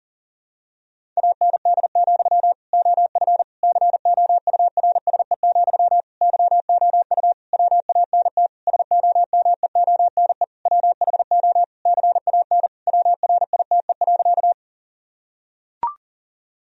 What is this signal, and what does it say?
Morse code, heard as a keyed tone, about 40 wpm, 700 Hz. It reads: AND, OF COURSE, YOU WANT SOMEONE WHO CAN WRITE.
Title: Sentences from Top 1000 Words - Review Part 1 - 40wpm